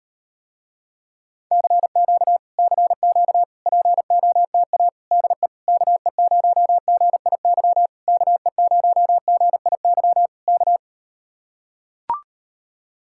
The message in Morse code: CQ CQ POTA DE KE0GIY KE0GIY K

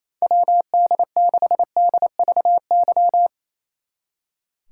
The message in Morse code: WD6DVY

